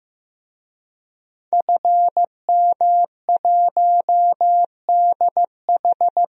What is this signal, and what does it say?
Morse code, heard as a keyed tone, about 15 wpm, 700 Hz. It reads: FM1DH
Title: DXCC-100 Call Signs - 5 Characters - Repeated 2x - 15wpm